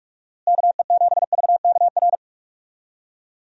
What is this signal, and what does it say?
Morse code, heard as a keyed tone, about 46 wpm, 700 Hz. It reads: KE7VKR
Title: USA Call Signs - 6 Characters - Repeated 3x - 45wpm